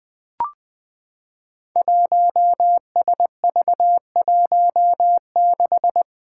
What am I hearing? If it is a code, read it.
Morse code, heard as a keyed tone, about 20 wpm, 700 Hz. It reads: 1SV16
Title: Five Challenging Letter-Number Combinations 20wpm